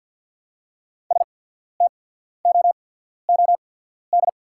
Morse code, heard as a keyed tone, about 48 wpm, spaced as for 14 wpm, 690 Hz. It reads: STOOD
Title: 8x Character Spacing - Top 500 Words - 50wpm